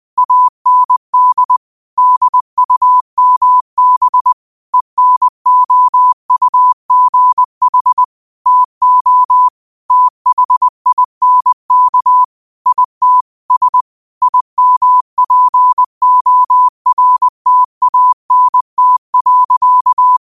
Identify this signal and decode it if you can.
Morse code, heard as a keyed tone, about 20 wpm, 1000 Hz. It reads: AND DUMB ENOUGH TO THINK IT S IMPORTANT.
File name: qotdfiles / 20wpm